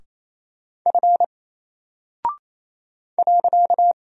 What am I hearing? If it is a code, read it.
Morse code, heard as a keyed tone, about 28 wpm, 700 Hz. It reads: F .
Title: Single Character (Letter:Number:Punctuation:Prosign) 28wpm